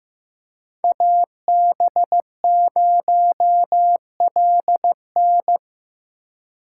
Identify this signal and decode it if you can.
Morse code, heard as a keyed tone, about 15 wpm, 700 Hz. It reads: AB0LN
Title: USA Call Signs - 5 Characters - 4x Character Spacing - 15wpm